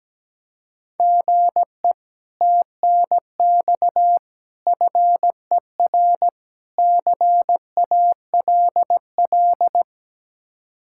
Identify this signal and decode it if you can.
Morse code, heard as a keyed tone, about 17 wpm, 700 Hz. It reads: GE TNX FER CALL
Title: Speed-Racing Common QSO Phrases - 17wpm